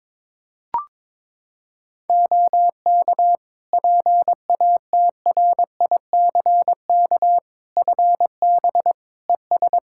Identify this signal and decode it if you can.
Morse code, heard as a keyed tone, about 22 wpm, 700 Hz. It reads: OK PATRICK FB ES
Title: Practice QSO - Exchange 2 - Info - 22wpm